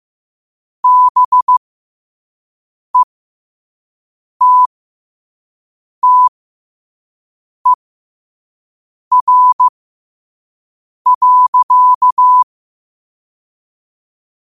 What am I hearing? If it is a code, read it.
Morse code, heard as a keyed tone, about 15 wpm, spaced as for 5 wpm, 1000 Hz. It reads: BETTER.
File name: qotdfiles / 5wpm